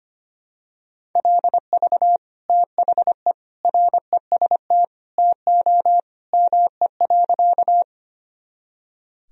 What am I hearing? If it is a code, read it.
Morse code, heard as a keyed tone, about 25 wpm, 700 Hz. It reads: LV THE REST TO ME.
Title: Sentences from Top 500 Words and Abbreviations 25wpm